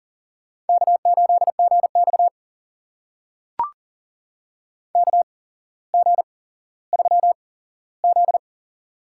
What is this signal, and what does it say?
Morse code, heard as a keyed tone, about 40 wpm, spaced as for 17 wpm, 700 Hz. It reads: K8GX KG3Z
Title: USA Call Signs - 8x Character Spacing - 40wpm